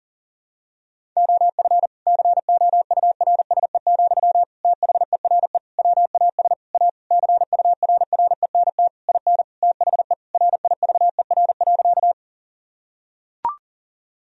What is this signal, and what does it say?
Morse code, heard as a keyed tone, about 40 wpm, 700 Hz. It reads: OF COURSE, THERE WAS A CURRENT IN THE RIVER.
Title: Sentences from Top 1000 Words - Review Part 4 - 40wpm